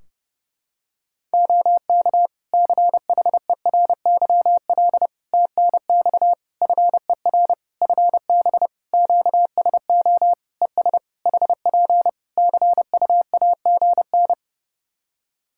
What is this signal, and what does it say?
Morse code, heard as a keyed tone, about 28 wpm, 700 Hz. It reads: OK CHERYL TNX FER FB QSO ES HP CUAGN
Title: Practice QSO - Exchange 4 - Ending - 30wpm